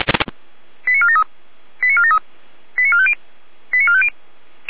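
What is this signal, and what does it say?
Five-tone selective call (ZVEI), heard as quick runs of five tones, tones of about 70 ms. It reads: 89473 89473 89450 89450
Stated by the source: ZVEI five-tone selective call received by SDR.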